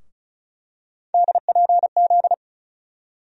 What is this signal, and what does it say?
Morse code, heard as a keyed tone, about 34 wpm, 700 Hz. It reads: DPZ
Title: USA Call Sign Postfixes - 3 Letters 35wpm